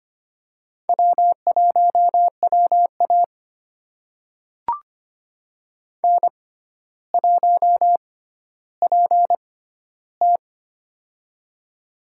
Morse code, heard as a keyed tone, about 25 wpm, spaced as for 13 wpm, 700 Hz. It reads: W1WA N1PT
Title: USA Call Signs - 4 Characters - 6x Character Spacing - 25wpm